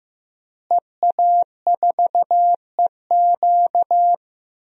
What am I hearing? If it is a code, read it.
Morse code, heard as a keyed tone, about 15 wpm, 700 Hz. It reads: EA4EQ
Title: DXCC-100 Call Signs - 5 Characters - Repeated 3x - 15wpm